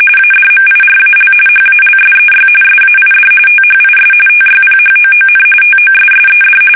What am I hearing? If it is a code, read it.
RTTY, about 45.45 baud. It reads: ADMIRAL HALSEY SAID THIS TYPHOON TOSSED
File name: fsk850 / battleshipnj